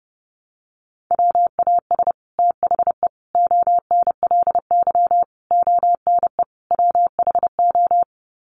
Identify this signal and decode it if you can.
Morse code, heard as a keyed tone, about 30 wpm, 700 Hz. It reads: WAS THE ONLY ONE WHO
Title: Sets of 5 Words Encore2 30wpm